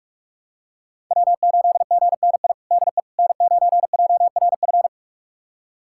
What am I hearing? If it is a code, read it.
Morse code, heard as a keyed tone, about 46 wpm, 700 Hz. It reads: W8GNI DE N9JRF
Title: USA Callsign DE Callsign - 5 Characters - 45wpm